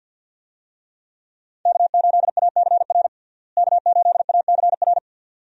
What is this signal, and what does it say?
Morse code, heard as a keyed tone, about 50 wpm, 700 Hz. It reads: K8ACR K8ACR
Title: USA Call Signs - 5 Characters - Repeated 2x - 50wpm